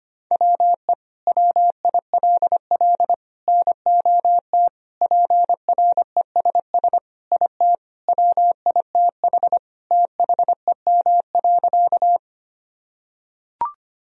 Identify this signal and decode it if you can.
Morse code, heard as a keyed tone, about 25 wpm, 700 Hz. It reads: WE WILL NOT PRESS IT WITH THEM.
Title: Sentences from Top 1000 Words - Review Part 1 - 25wpm